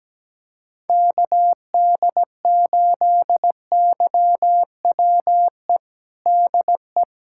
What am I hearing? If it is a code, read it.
Morse code, heard as a keyed tone, about 17 wpm, 700 Hz. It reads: KD8YWE DE
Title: USA Callsign DE Callsign 17wpm